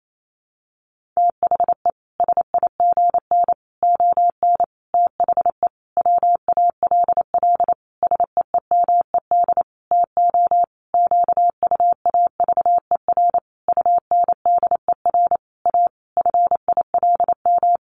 Morse code, heard as a keyed tone, about 28 wpm, 700 Hz. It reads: THE SIGN ON THE WALL SEEMED TO QUAVER UNDER A FILM
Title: A Sound Of Thunder 28wpm